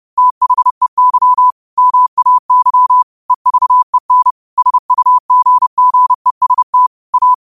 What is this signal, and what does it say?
Morse code, heard as a keyed tone, about 30 wpm, 1000 Hz. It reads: THEY MAY EVEN SUGGEST A
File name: 30wpm